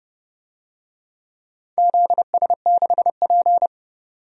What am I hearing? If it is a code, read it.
Morse code, heard as a keyed tone, about 30 wpm, 700 Hz. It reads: ZS6P
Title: DXCC-100 Call Signs - 4 Characters - 30wpm